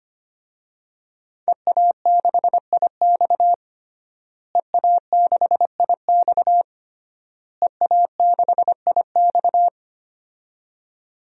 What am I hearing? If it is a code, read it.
Morse code, heard as a keyed tone, about 25 wpm, 700 Hz. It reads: EA6IX EA6IX EA6IX
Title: DXCC-100 Call Signs - 5 Characters - Repeated 3x - 25wpm